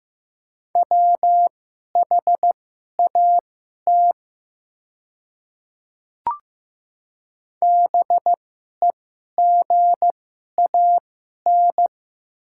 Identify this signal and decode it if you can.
Morse code, heard as a keyed tone, about 15 wpm, spaced as for 11 wpm, 690 Hz. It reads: WHAT BEGAN
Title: 2x Character Spacing - Top 300 Words - 15wpm